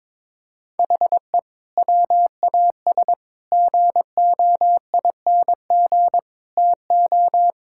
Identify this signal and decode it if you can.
Morse code, heard as a keyed tone, about 22 wpm, 700 Hz. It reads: HE WAS GOING TO
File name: practice / 22wpm / Sentences with 4-Word Phrases - 22wpm